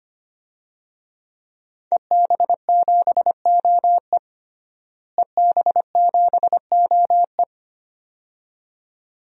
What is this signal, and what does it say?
Morse code, heard as a keyed tone, about 25 wpm, 700 Hz. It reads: EB7OE EB7OE
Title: DXCC-100 Call Signs - 5 Characters - Repeated 2x - 25wpm